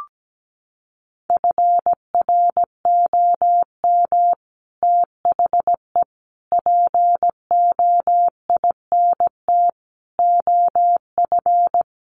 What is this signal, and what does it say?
Morse code, heard as a keyed tone, about 17 wpm, 700 Hz. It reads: FROM THE POINT OF
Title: Sets of 5 Words 17wpm